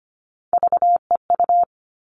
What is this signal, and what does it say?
Morse code, heard as a keyed tone, about 25 wpm, 700 Hz. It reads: VEU
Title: All Cut Numbers - 3-Digit 25wpm